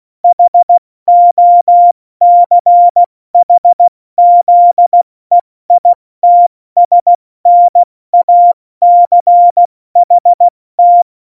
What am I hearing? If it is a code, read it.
Morse code, heard as a keyed tone, about 16 wpm, spaced as for 14 wpm, 700 Hz. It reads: HOCHZEITSNACHT
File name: Morsekurs / Witze / Witz 25. Juni 2019, 70 F80